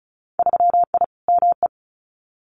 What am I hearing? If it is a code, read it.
Morse code, heard as a keyed tone, about 36 wpm, 700 Hz. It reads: 3I ME
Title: Winter Field Day Exchanges 35wpm